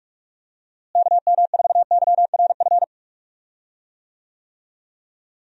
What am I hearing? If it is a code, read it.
Morse code, heard as a keyed tone, about 46 wpm, 700 Hz. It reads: KM4YRF
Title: USA Call Signs - 6 Characters - Repeated 3x - 45wpm